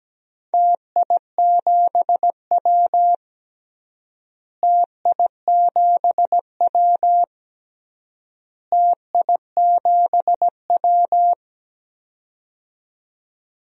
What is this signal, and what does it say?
Morse code, heard as a keyed tone, about 17 wpm, 700 Hz. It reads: TI7W TI7W TI7W
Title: DXCC-100 Call Signs - 4 Characters - Repeated 3x - 17wpm